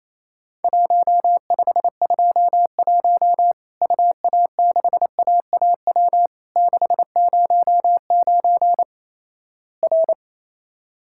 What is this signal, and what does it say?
Morse code, heard as a keyed tone, about 28 wpm, 700 Hz. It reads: 1521 UA6AAW 609 R
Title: Worked All Europe DX Contest - QTC Exchange - 28wpm